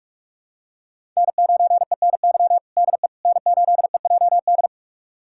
Morse code, heard as a keyed tone, about 46 wpm, 700 Hz. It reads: N9ENY DE N8EJD
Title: USA Callsign DE Callsign - 5 Characters - 45wpm